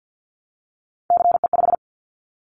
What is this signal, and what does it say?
Morse code, heard as a keyed tone, about 50 wpm, 700 Hz. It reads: CE5
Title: Callsign Prefixes for Top 100 DXCC - 50wpm